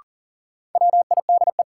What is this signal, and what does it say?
Morse code, heard as a keyed tone, about 40 wpm, 700 Hz. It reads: WIDE
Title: Sentences from Top 900 Words - New Word Drill - 40wpm